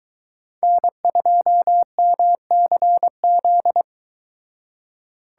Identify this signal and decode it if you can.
Morse code, heard as a keyed tone, about 23 wpm, 700 Hz. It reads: N2MCZ